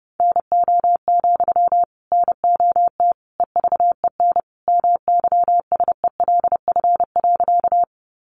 Morse code, heard as a keyed tone, about 30 wpm, 700 Hz. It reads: NO, NOT EVEN MYSELF.
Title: Sentences from Top 1300 Words 30wpm